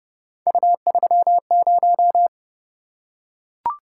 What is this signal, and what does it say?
Morse code, heard as a keyed tone, about 29 wpm, 700 Hz. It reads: U30